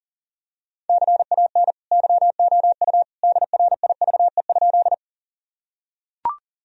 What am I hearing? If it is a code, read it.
Morse code, heard as a keyed tone, about 40 wpm, 700 Hz. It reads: CAN YOU DRIVE?
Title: Sentences from Top 1000 Words - Review Part 2 - 40wpm